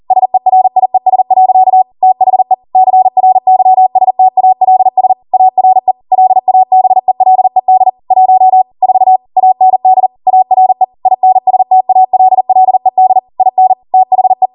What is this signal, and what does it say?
Morse code, heard as a keyed tone, about 40 wpm, 750 Hz. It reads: SERIES. THE CRYSTALS ARE LABELED 1 4 AND ARE INSTALLED IN THE
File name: Morse / 40WPM